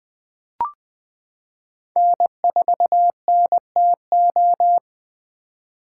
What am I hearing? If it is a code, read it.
Morse code, heard as a keyed tone, about 20 wpm, 700 Hz. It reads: N4NTO
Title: 2023 POTA Activator Call Signs - 20wpm